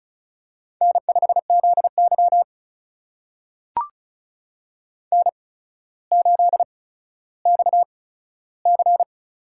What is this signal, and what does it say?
Morse code, heard as a keyed tone, about 36 wpm, spaced as for 15 wpm, 700 Hz. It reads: N5ZY N8XC
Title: USA Call Signs - 4 Characters - 8x Character Spacing - 35wpm